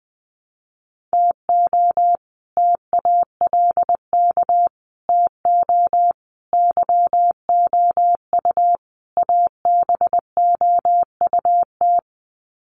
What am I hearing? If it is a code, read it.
Morse code, heard as a keyed tone, about 20 wpm, 700 Hz. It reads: TO TALK TO YOU ABOUT
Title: Sets of 5 Words Encore 20wpm